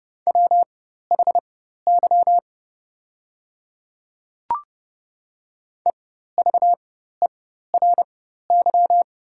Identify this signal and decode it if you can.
Morse code, heard as a keyed tone, about 30 wpm, spaced as for 14 wpm, 700 Hz. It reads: WHY EVERY
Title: 4x Character Spacing - Top 200 Words - 30wpm